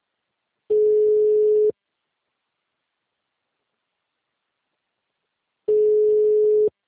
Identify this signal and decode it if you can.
Ringback tone (Europe, 425 Hz).
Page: Itu